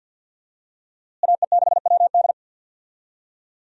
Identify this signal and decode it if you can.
Morse code, heard as a keyed tone, about 52 wpm, 700 Hz. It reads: AE6WD
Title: USA Call Signs - 5 Characters - Repeated 3x - 50wpm